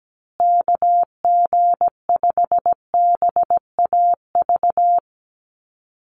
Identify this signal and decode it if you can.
Morse code, heard as a keyed tone, about 17 wpm, 700 Hz. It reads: KG5BAV
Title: USA Call Signs Repeated 2x 17wpm